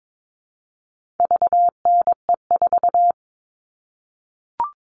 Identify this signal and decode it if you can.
Morse code, heard as a keyed tone, about 22 wpm, 700 Hz. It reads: VNE4
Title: All Cut Numbers - 4-Digit 22wpm